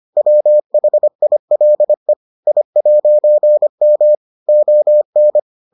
Morse code, heard as a keyed tone, about 25 wpm, 600 Hz. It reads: WHILE I'M ON